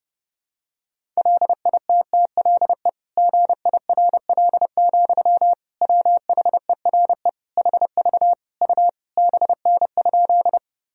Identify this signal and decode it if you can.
Morse code, heard as a keyed tone, about 32 wpm, 700 Hz. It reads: LITTLE GIRL, WHERE HV U BN?
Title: Sentences from Top 400 Words and Abbreviations 30wpm